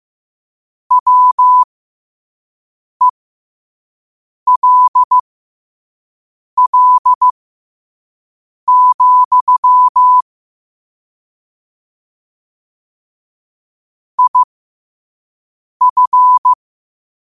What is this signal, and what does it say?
Morse code, heard as a keyed tone, about 15 wpm, spaced as for 5 wpm, 1000 Hz. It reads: WELL, IF
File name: 2026-03-16-5wpm